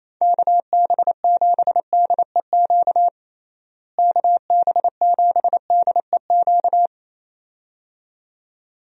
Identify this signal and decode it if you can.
Morse code, heard as a keyed tone, about 28 wpm, 700 Hz. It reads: KB7DEQ KB7DEQ
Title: USA Call Signs - 6 Characters - Repeated 2x - 28wpm